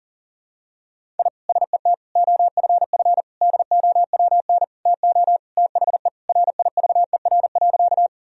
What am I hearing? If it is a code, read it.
Morse code, heard as a keyed tone, about 40 wpm, 700 Hz. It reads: I SET OFF DOWN TO THE RIVER.
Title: Sentences from Top 1000 Words - Review Part 1 - 40wpm